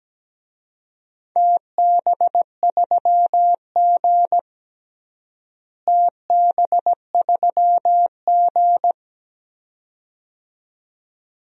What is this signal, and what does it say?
Morse code, heard as a keyed tone, about 17 wpm, 700 Hz. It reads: TB3G TB3G
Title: DXCC-100 Call Signs - 4 Characters - Repeated 2x - 17wpm